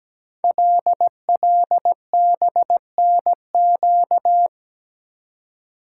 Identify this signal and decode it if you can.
Morse code, heard as a keyed tone, about 17 wpm, 700 Hz. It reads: LLBNQ
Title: Code Groups - 5 Letters - 17wpm